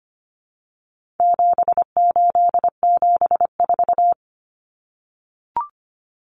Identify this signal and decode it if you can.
Morse code, heard as a keyed tone, about 25 wpm, 700 Hz. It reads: 7874
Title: Common Cut Numbers - 4-Digit 25wpm